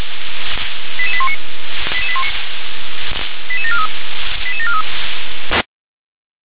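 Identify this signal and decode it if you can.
Five-tone selective call (ZVEI), heard as quick runs of five tones, tones of about 70 ms. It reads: 88819 88743 88743
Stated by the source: ZVEI five-tone call received on SDR.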